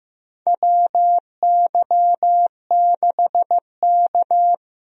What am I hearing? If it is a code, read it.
Morse code, heard as a keyed tone, about 15 wpm, 700 Hz. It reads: WY6K
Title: USA Call Signs - 4 Characters - Repeated 3x - 15wpm